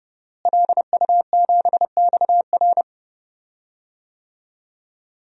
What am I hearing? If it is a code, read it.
Morse code, heard as a keyed tone, about 30 wpm, 700 Hz. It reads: LU7XR
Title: DXCC-100 Call Signs - 5 Characters - Repeated 3x - 30wpm